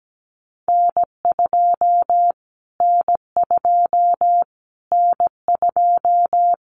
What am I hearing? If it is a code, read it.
Morse code, heard as a keyed tone, about 17 wpm, 700 Hz. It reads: N2 N2 N2
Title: Callsign Prefixes for Top 25 DXCC - Repeated 3x - 17wpm